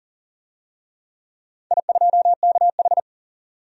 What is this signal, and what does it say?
Morse code, heard as a keyed tone, about 40 wpm, 700 Hz. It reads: I2KH
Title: DXCC-100 Call Signs - 4 Characters - 40wpm